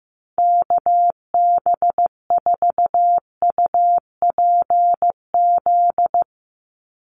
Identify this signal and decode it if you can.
Morse code, heard as a keyed tone, about 15 wpm, 700 Hz. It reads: KB4UPZ
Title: USA Call Signs 15wpm